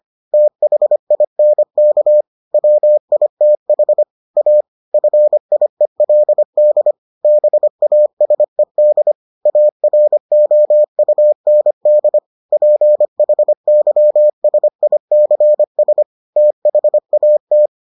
Morse code, heard as a keyed tone, about 25 wpm, 600 Hz. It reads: THINK WITH A FIELD BASED AROUND PHYSICS THAT